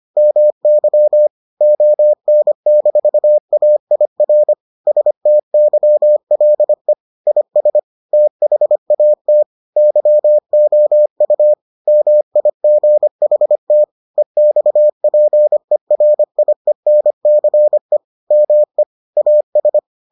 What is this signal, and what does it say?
Morse code, heard as a keyed tone, about 25 wpm, 600 Hz. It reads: MY ON-AIR STYLE IS THAT YOU MIGHT EXPERIENCE ME AS